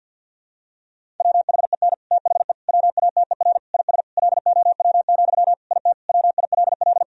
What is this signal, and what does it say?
Morse code, heard as a keyed tone, about 50 wpm, 700 Hz. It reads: WHEN THE WATER IS LOW, IT WILL
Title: Sentences from Top 1000 Words - Review Part 1 - 50wpm